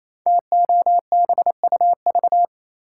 Morse code, heard as a keyed tone, about 29 wpm, 700 Hz. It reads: TOBUV
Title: Code Groups - 5 Letters - 28wpm